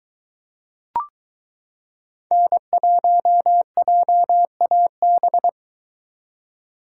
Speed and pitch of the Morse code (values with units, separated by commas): 23 wpm, 700 Hz